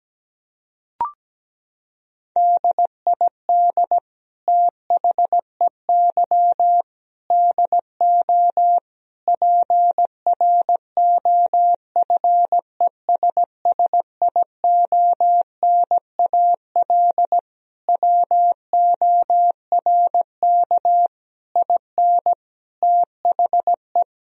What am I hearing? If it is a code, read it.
Morse code, heard as a keyed tone, about 17 wpm, 700 Hz. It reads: DID THEY DO PROFESSIONAL WORK IN THE
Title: Sentences from Top 1000 Words - Review Part 4 - 17wpm